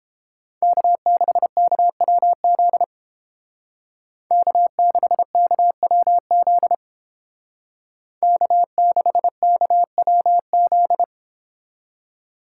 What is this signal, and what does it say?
Morse code, heard as a keyed tone, about 31 wpm, 700 Hz. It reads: K6KWZ K6KWZ K6KWZ